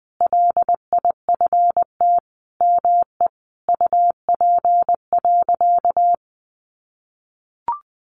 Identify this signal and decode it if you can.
Morse code, heard as a keyed tone, about 20 wpm, 700 Hz. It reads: LIFT ME UP.
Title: Sentences from Top 1000 Words 20wpm